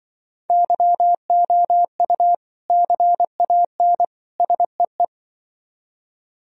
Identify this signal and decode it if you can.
Morse code, heard as a keyed tone, about 24 wpm, 700 Hz. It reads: YOU CAN SEE
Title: Speed-Racing - Sets of 3 Words 20wpm